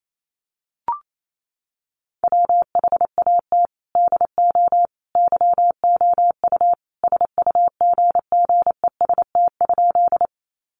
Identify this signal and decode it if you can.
Morse code, heard as a keyed tone, about 28 wpm, 700 Hz. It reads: WHAT DO YOU SUGGEST?
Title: Sentences from Top 1300 Words 28wpm